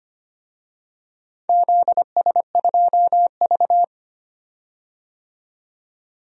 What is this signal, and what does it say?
Morse code, heard as a keyed tone, about 25 wpm, 700 Hz. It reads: ZS2V
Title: DXCC-100 Call Signs - 4 Characters - 25wpm